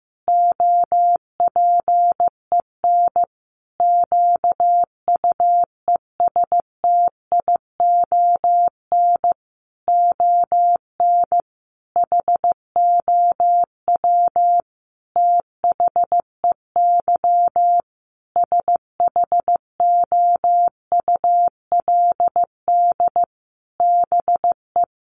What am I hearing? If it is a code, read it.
Morse code, heard as a keyed tone, about 15 wpm, 700 Hz. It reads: OPEN QUESTION ON HOW THEY SHOULD BE
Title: Sentences from Top 1300 Words 15wpm